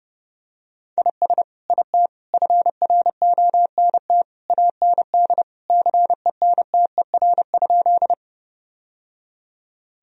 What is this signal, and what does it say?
Morse code, heard as a keyed tone, about 30 wpm, 700 Hz. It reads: IS IT FRONT AND CENTER?
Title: Sentences from Top 1000 Words - Review Part 2 - 30wpm